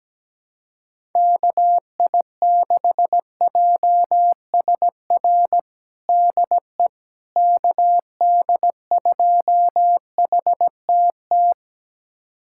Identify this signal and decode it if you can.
Morse code, heard as a keyed tone, about 17 wpm, 700 Hz. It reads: KI6JSR DE KD2HTT
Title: USA Callsign DE Callsign - 6 Characters - 17wpm